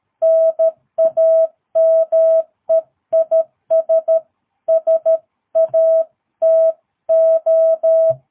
Morse code, heard as a keyed tone, about 13 wpm, 650 Hz. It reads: NAME IS SATO
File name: jh8cbh / 2022cw